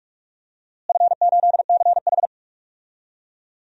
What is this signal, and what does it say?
Morse code, heard as a keyed tone, about 46 wpm, 700 Hz. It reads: F8CH
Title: DXCC-100 Call Signs - 4 Characters - Repeated 3x - 45wpm